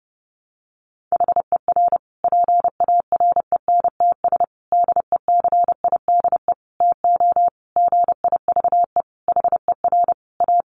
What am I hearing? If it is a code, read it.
Morse code, heard as a keyed tone, about 30 wpm, 700 Hz. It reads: HER PARENTS DECIDE TO GIVE HER A
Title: Easy Phrases 30wpm